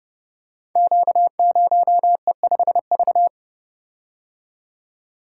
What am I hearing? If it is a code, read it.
Morse code, heard as a keyed tone, about 30 wpm, 700 Hz. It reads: Q0E5V
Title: Code Groups - 5 Letter:Numbers - 30wpm